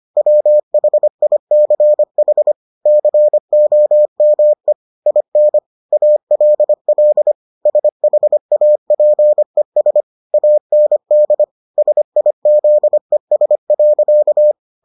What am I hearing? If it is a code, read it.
Morse code, heard as a keyed tone, about 25 wpm, 600 Hz. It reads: WHICH COME IN ALL SHAPES AND SIZES.